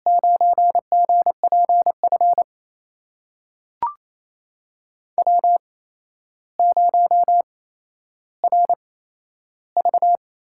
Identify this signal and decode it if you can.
Morse code, heard as a keyed tone, about 28 wpm, spaced as for 12 wpm, 700 Hz. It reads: 9GPF W0RV
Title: USA Call Signs - 5 Characters - 8x Character Spacing - 28wpm